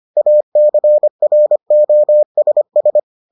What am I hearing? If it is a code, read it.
Morse code, heard as a keyed tone, about 25 wpm, 600 Hz. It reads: ACROSS